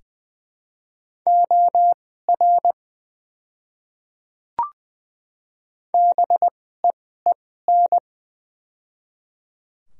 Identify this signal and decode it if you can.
Morse code, heard as a keyed tone, about 20 wpm, spaced as for 14 wpm, 700 Hz. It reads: OR BEEN
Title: 2x Character Spacing - Top 100 Words - 20wpm